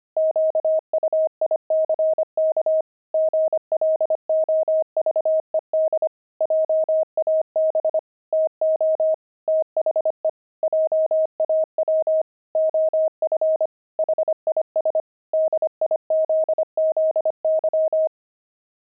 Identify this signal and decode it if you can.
Morse code, heard as a keyed tone, about 25 wpm, 630 Hz. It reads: QUICK GLOVED JAB TO THE JAW OF HIS DIZZY
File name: pangram-fs-25-cs-25-gloved